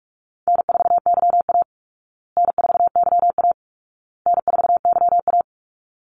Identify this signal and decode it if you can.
Morse code, heard as a keyed tone, about 44 wpm, 700 Hz. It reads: N4YA N4YA N4YA
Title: Easy Call Signs Repeated 3x - 45wpm